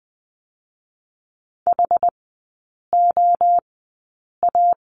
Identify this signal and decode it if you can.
Morse code, heard as a keyed tone, about 20 wpm, 700 Hz. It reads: H O A
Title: 218 Lesson 35e - Number 0 - Three Characters 20wpm